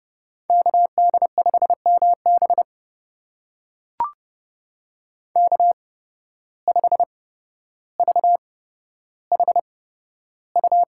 Morse code, heard as a keyed tone, about 30 wpm, spaced as for 13 wpm, 700 Hz. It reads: KD5MB K5VHU
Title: USA Call Signs - 5 Characters - 8x Character Spacing - 30wpm